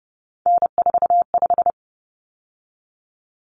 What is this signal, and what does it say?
Morse code, heard as a keyed tone, about 30 wpm, 700 Hz. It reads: N45
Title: Common Cut Numbers - 3-Digit 30wpm